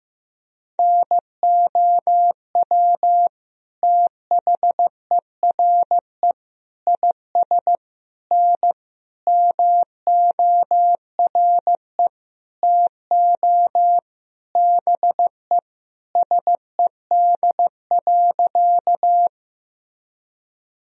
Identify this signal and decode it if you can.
Morse code, heard as a keyed tone, about 16 wpm, 700 Hz. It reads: NOW THERE IS N MORE TO BE SED.
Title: Sentences from Top 100 Words and Abbreviations 15wpm